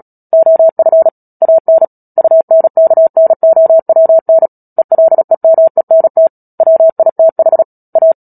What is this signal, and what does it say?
Morse code, heard as a keyed tone, about 37 wpm, 650 Hz. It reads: OF AN UNKNOWN ELEMENT WITH A
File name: Book Two / 10-Chapter 10 - Epilogue - 36 WPM